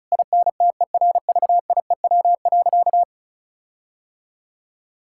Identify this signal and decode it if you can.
Morse code, heard as a keyed tone, about 34 wpm, 700 Hz. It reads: INTERVIEW.
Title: Sentences from Top 1600 Words 35wpm